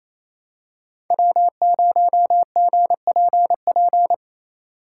Morse code, heard as a keyed tone, about 28 wpm, 700 Hz. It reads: W0GPP
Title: USA Call Signs - 5 Characters - 4x Character Spacing - 28wpm